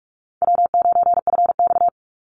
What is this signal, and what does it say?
Morse code, heard as a keyed tone, about 44 wpm, 700 Hz. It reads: W9FX
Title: Call Signs Repeated 3x - 45wpm